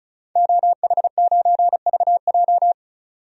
Morse code, heard as a keyed tone, about 34 wpm, 700 Hz. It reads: OH9VJ
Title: DXCC-100 Call Signs - 5 Characters - 35wpm